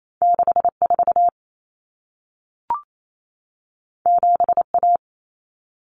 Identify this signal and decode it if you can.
Morse code, heard as a keyed tone, about 28 wpm, 700 Hz. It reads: 64 7A
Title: Common Cut Numbers - 2-Digit 28wpm